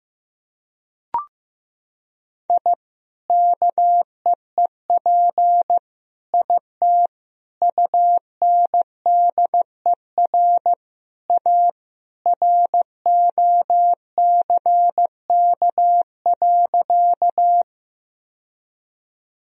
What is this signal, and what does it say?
Morse code, heard as a keyed tone, about 15 wpm, 700 Hz. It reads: I KEEP IT UNDER A ROCK.
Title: Sentences from Top 1000 Words - Review Part 2 - 15wpm